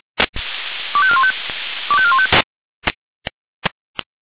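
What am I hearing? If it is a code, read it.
Five-tone selective call (ZVEI), heard as quick runs of five tones, tones of about 70 ms. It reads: 25415 25415
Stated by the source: ZVEI five-tone selective call received by SDR.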